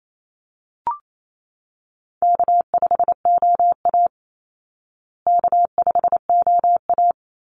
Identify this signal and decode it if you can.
Morse code, heard as a keyed tone, about 28 wpm, 700 Hz. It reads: K5OA K5OA
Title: Easy Call Signs Repeated 2x - 28wpm